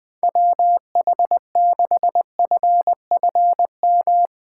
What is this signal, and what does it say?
Morse code, heard as a keyed tone, about 21 wpm, 700 Hz. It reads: WH6FFM
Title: USA Call Signs - 3x Character Spacing - 20wpm